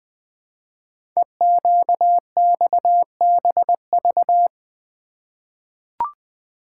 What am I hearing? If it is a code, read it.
Morse code, heard as a keyed tone, about 20 wpm, 700 Hz. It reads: EQXBV
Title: Code Groups - 5 Letter:Numbers - 20wpm